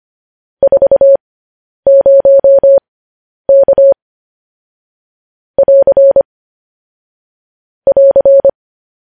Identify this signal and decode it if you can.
Morse code, heard as a keyed tone, about 25 wpm, spaced as for 10 wpm, 550 Hz. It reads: 40K <AR> <AR>
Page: news headlines in morse code at 10 wpm